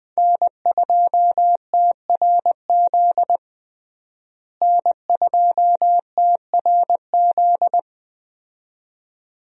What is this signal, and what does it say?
Morse code, heard as a keyed tone, about 20 wpm, 700 Hz. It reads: N2TRZ N2TRZ
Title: USA Call Signs - 5 Characters - Repeated 2x - 20wpm